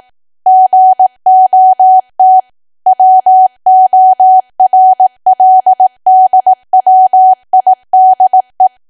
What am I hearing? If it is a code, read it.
Morse code, heard as a keyed tone, about 18 wpm, 750 Hz. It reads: GOT WORLDWIDE
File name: Archive / 18 WPM / 18WPM